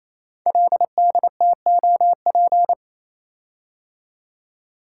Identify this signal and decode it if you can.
Morse code, heard as a keyed tone, about 27 wpm, 700 Hz. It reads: LDTOP
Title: Code Groups - 5 Letters - 28wpm